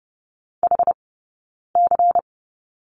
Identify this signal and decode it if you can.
Morse code, heard as a keyed tone, about 30 wpm, 700 Hz. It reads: H C
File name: 092 Lesson 16d - Letter c - Two Characters 30wpm